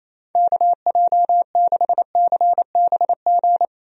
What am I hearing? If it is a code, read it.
Morse code, heard as a keyed tone, about 28 wpm, 700 Hz. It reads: KJ6CBG
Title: USA Call Signs - 6 Characters - 28wpm